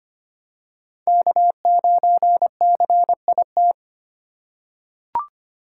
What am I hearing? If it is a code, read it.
Morse code, heard as a keyed tone, about 25 wpm, 700 Hz. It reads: K9CIT
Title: USA Call Signs - 5 Characters - 2x Character Spacing - 25wpm